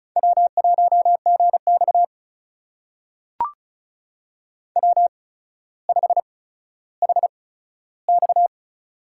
Morse code, heard as a keyed tone, about 36 wpm, spaced as for 15 wpm, 700 Hz. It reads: W1GX W5HX
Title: USA Call Signs - 8x Character Spacing - 35wpm